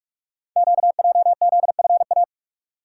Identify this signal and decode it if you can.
Morse code, heard as a keyed tone, about 44 wpm, 700 Hz. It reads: QJZFA
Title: Code Groups - 5 Letters - 45wpm